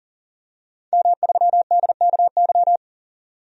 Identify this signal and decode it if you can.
Morse code, heard as a keyed tone, about 40 wpm, 700 Hz. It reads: M3DKY
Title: DXCC-100 Call Signs - 5 Characters - 40wpm